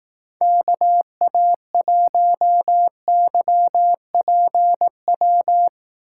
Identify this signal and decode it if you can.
Morse code, heard as a keyed tone, about 18 wpm, 700 Hz. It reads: KA1YPW
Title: Speed Racing - USA 2x3 Call Signs 15wpm